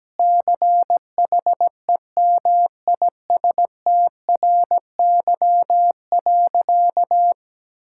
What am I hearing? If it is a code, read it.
Morse code, heard as a keyed tone, about 17 wpm, 700 Hz. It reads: CHEMISTRY.
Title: Sentences from Top 1000 Words - Review Part 6 - 17wpm